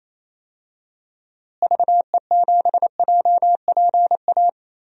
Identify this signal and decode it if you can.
Morse code, heard as a keyed tone, about 28 wpm, 700 Hz. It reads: VE7JPA
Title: DXCC-100 Call Signs - 6 Characters - Repeated 2x - 28wpm